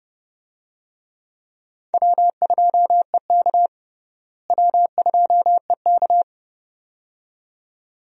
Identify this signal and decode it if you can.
Morse code, heard as a keyed tone, about 32 wpm, 700 Hz. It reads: W2EK W2EK
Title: USA Call Signs - 4 Characters - Repeated 2x - 30wpm